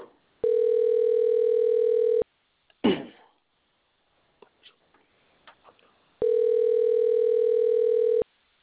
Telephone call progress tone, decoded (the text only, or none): ringback tone